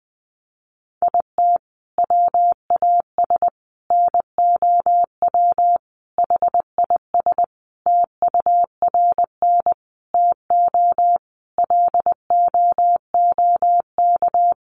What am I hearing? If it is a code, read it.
Morse code, heard as a keyed tone, about 20 wpm, 700 Hz. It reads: IT WAS NOW HIS TURN TO LOOK
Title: Sentences from Top 200 Words 20wpm